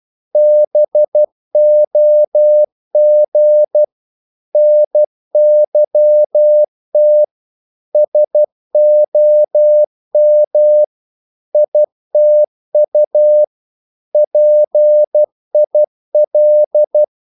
Morse code, heard as a keyed tone, about 12 wpm, 600 Hz. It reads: BOG NYT SOM ITU PIL
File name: Korte ord DK 12wpm